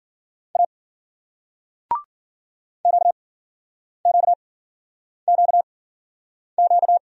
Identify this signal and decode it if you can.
Morse code, heard as a keyed tone, about 53 wpm, 700 Hz. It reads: A Q Q Q Q